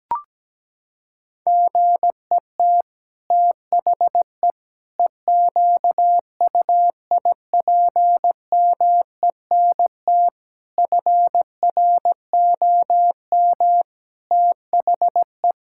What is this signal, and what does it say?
Morse code, heard as a keyed tone, about 17 wpm, 700 Hz. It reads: GET THE EQUIPMENT FROM THE
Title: Sentences from Top 1000 Words - Review Part 5 - 17wpm